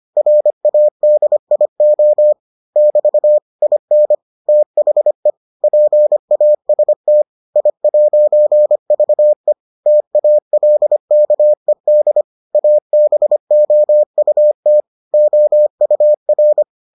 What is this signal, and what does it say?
Morse code, heard as a keyed tone, about 25 wpm, 600 Hz. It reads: RADIO <BT> IN THE PAST I'VE TALKED ABOUT OUR